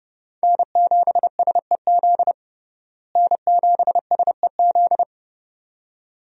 Morse code, heard as a keyed tone, about 30 wpm, 700 Hz. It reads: N7SEZ N7SEZ
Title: USA Call Signs - 5 Characters - Repeated 2x - 30wpm